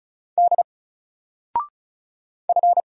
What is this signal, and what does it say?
Morse code, heard as a keyed tone, about 35 wpm, 700 Hz. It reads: D F
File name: Single-Letter-Number-35wpm